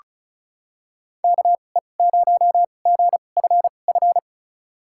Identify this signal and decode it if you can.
Morse code, heard as a keyed tone, about 34 wpm, spaced as for 25 wpm, 700 Hz. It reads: KE0GFF
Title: USA Call Signs - 6 Characters - 2x Character Spacing - 35wpm